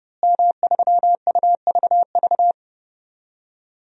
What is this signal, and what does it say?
Morse code, heard as a keyed tone, about 30 wpm, 700 Hz. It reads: M3UVV
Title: DXCC-100 Call Signs - 5 Characters - Repeated 3x - 30wpm